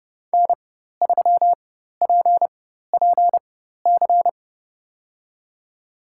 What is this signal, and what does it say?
Morse code, heard as a keyed tone, about 32 wpm, spaced as for 14 wpm, 700 Hz. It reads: N3PPC
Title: USA Call Signs - 5 Characters - 4x Character Spacing - 30wpm